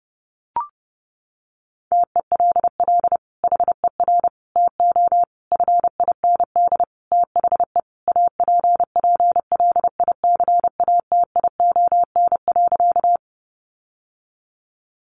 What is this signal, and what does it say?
Morse code, heard as a keyed tone, about 30 wpm, 700 Hz. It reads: TELL HER TO FIND THE APPLICATION.
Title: Sentences from Top 1200 Words 30wpm